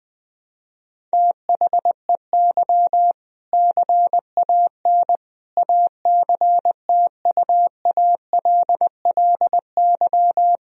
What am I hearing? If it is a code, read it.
Morse code, heard as a keyed tone, about 20 wpm, 700 Hz. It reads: THEY CAN ACTUALLY
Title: Sentences from Top 1000 Words - Review Part 4 - 20wpm